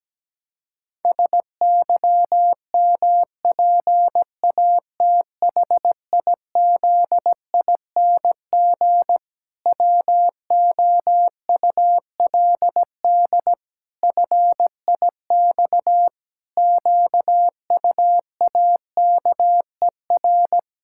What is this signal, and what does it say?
Morse code, heard as a keyed tone, about 17 wpm, 700 Hz. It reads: SYMPATHIZING WOULD FIX QUAKER
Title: Pangram Challenge v2 - 17wpm